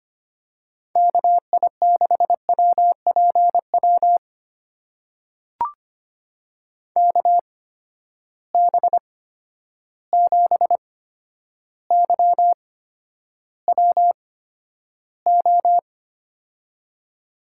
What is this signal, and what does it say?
Morse code, heard as a keyed tone, about 25 wpm, spaced as for 11 wpm, 700 Hz. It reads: KI6WPW KB7YWO
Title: USA Call Signs - 6 Characters - 8x Character Spacing - 25wpm